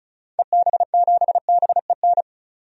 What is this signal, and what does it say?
Morse code, heard as a keyed tone, about 34 wpm, 700 Hz. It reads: EB7BEN
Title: DXCC-100 Call Signs - 6 Characters - 35wpm